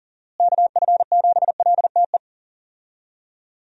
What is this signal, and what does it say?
Morse code, heard as a keyed tone, about 40 wpm, 700 Hz. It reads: KF7LTE
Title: USA Call Signs - 6 Characters - Repeated 2x - 40wpm